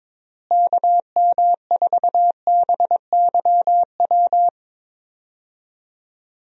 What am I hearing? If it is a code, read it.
Morse code, heard as a keyed tone, about 21 wpm, 700 Hz. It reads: KM4BYW
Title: USA Call Signs - 6 Characters - Repeated 3x - 22wpm